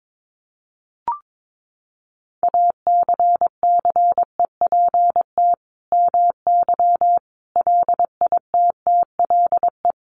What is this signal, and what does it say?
Morse code, heard as a keyed tone, about 22 wpm, 700 Hz. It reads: ACCEPT MY LITTLE
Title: Sentences from Top 1300 Words 22wpm